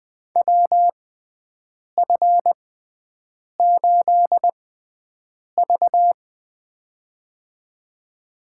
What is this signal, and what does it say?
Morse code, heard as a keyed tone, about 20 wpm, spaced as for 7 wpm, 700 Hz. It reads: WF8V
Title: USA Call Signs - 4 Characters - 6x Character Spacing - 20wpm